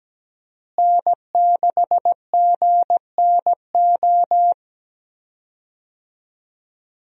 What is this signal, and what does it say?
Morse code, heard as a keyed tone, about 17 wpm, 700 Hz. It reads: N6GNO
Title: USA Call Signs - 5 Characters - Repeated 3x - 17wpm